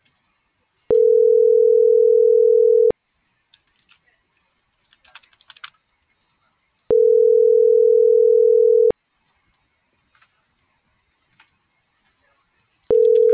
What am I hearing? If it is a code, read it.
Ringback tone (North America), 440 + 480 Hz.